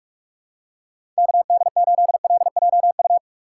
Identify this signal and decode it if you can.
Morse code, heard as a keyed tone, about 47 wpm, 700 Hz. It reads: KD8LJU
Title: USA Call Signs - 6x Character Spacing - 45wpm